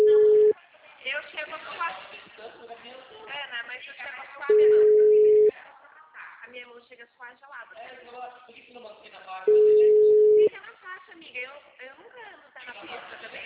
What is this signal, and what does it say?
Ringback tone (Europe, 425 Hz), with speech.